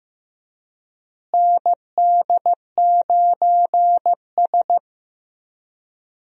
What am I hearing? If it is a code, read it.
Morse code, heard as a keyed tone, about 15 wpm, 700 Hz. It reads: ND9S
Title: USA Call Signs - 4 Characters - Repeated 3x - 15wpm